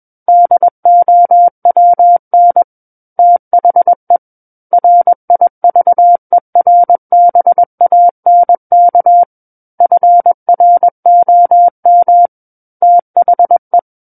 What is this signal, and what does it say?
Morse code, heard as a keyed tone, about 21 wpm, 700 Hz. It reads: DOWN THE RIVERBANK FROM THE
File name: Book One / 12-Chapter 12 - 21 WPM